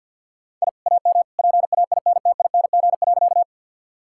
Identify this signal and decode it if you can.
Morse code, heard as a keyed tone, about 50 wpm, 700 Hz. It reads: I AM PAINTING.
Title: Sentences from Top 1000 Words - Review Part 6 - 50wpm